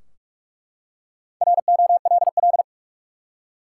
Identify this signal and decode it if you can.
Morse code, heard as a keyed tone, about 47 wpm, 700 Hz. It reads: ROLL
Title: Four Letter Words - 45wpm